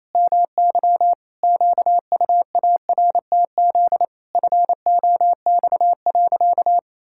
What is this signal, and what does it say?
Morse code, heard as a keyed tone, about 28 wpm, 700 Hz. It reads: MY QUARTZ FOX.
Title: Pangram Challenge v2 - 28wpm